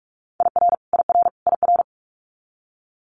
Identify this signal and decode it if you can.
Morse code, heard as a keyed tone, about 46 wpm, 700 Hz. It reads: IR IR IR
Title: Multi-letter Phonemes Repeated 3x 45wpm